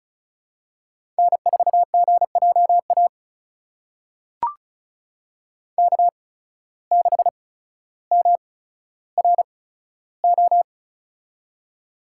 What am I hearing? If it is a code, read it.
Morse code, heard as a keyed tone, about 36 wpm, spaced as for 15 wpm, 700 Hz. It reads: N4GJA K6MRO
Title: USA Call Signs - 5 Characters - 8x Character Spacing - 35wpm